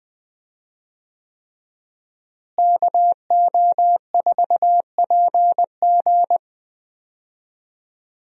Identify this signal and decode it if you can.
Morse code, heard as a keyed tone, about 21 wpm, 700 Hz. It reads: KO4PG
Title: USA Call Signs - 5 Characters - Repeated 3x - 20wpm